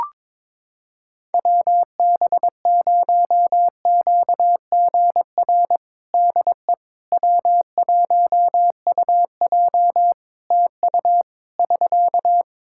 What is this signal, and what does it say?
Morse code, heard as a keyed tone, about 22 wpm, 700 Hz. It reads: WB0QGR DE W1UJ TU <SK>
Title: Practice QSO - Exchange 4 - Ending - 22wpm